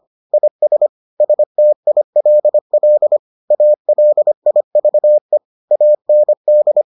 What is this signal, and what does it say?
Morse code, heard as a keyed tone, about 25 wpm, 600 Hz. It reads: IS STILL ALIVE AND